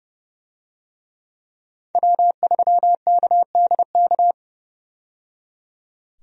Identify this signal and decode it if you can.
Morse code, heard as a keyed tone, about 30 wpm, 700 Hz. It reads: W3KDK
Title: USA Call Signs - 5 Characters - 30wpm